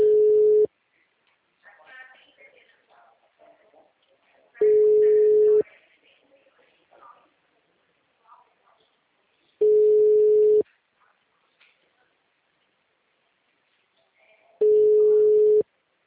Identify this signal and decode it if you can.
Ringback tone (Europe, 425 Hz), with speech.
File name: out- -2012- - - .7741